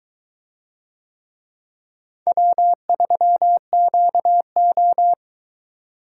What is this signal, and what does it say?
Morse code, heard as a keyed tone, about 23 wpm, 700 Hz. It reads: W3QO